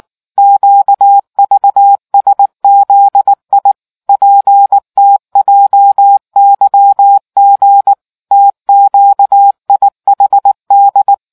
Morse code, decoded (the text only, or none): QVSZI PTJYG TQIHD